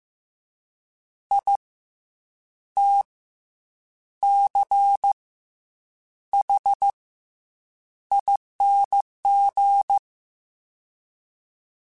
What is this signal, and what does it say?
Morse code, heard as a keyed tone, about 15 wpm, 800 Hz. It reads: I T C H ING
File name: Special INGs / Special INGs-mp3s / SlowToFaster ing @13